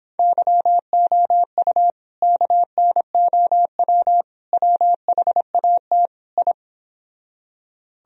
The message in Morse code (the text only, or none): YOU KNOW WHAT I